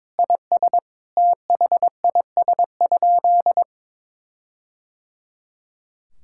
Morse code, heard as a keyed tone, about 22 wpm, 700 Hz. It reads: IS THIS?
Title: Sentences from Top 1600 Words 22wpm